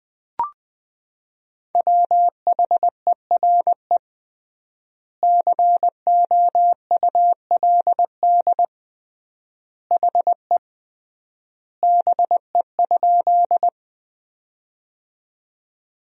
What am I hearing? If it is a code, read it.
Morse code, heard as a keyed tone, about 20 wpm, 700 Hz. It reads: WHERE COULD HE BE?
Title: 3x Word-Spacing - Sentences from Top 200 Words - 20wpm